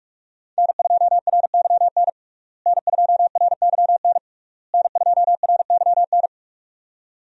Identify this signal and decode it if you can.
Morse code, heard as a keyed tone, about 45 wpm, 700 Hz. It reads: N2RYN N2RYN N2RYN
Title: USA Call Signs - 5 Characters - Repeated 3x - 45wpm